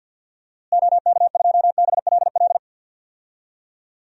Morse code, heard as a keyed tone, about 52 wpm, 700 Hz. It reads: OK2BLL
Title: DXCC-100 Call Signs - 6 Characters - Repeated 3x - 50wpm